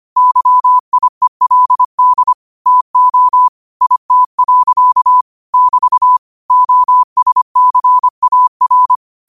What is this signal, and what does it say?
Morse code, heard as a keyed tone, about 25 wpm, 1000 Hz. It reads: YIELD TO IT. <BT> OSCAR